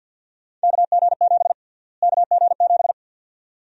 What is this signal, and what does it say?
Morse code, heard as a keyed tone, about 50 wpm, 700 Hz. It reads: KG7 KG7
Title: USA Call Sign Prefixes - Repeated 2x - 50wpm